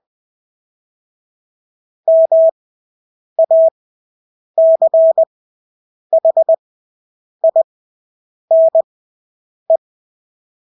Morse code, heard as a keyed tone, about 20 wpm, spaced as for 8 wpm, 650 Hz. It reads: MACHINE
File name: Engelske ord 20/8 wpm